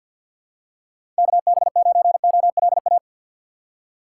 Morse code, heard as a keyed tone, about 51 wpm, 700 Hz. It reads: KB9OLA
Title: USA Call Signs - 6 Characters - 3x Character Spacing - 50wpm